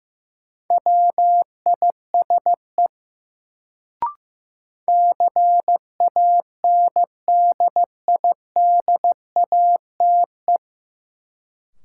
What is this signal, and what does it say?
Morse code, heard as a keyed tone, about 15 wpm, 700 Hz. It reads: WISE CANDIDATE
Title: Sentences from Top 1600 Words - New Word Drill Alt - 15wpm